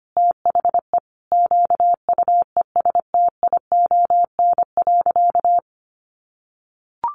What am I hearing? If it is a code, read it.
Morse code, heard as a keyed tone, about 25 wpm, 700 Hz. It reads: THE QUESTION.
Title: Sentences from Top 1500 Words 25wpm